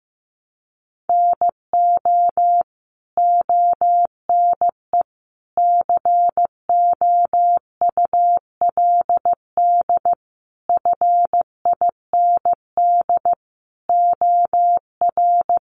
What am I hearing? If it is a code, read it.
Morse code, heard as a keyed tone, about 15 wpm, 700 Hz. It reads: NO ONE COULD FIND OR
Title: Sentences from Top 1100 Words 15wpm